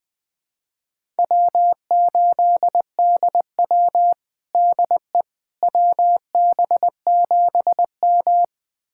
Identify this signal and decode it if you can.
Morse code, heard as a keyed tone, about 20 wpm, 700 Hz. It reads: W8DW DE WB7M
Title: USA Callsign DE Callsign - 4 Characters - 20wpm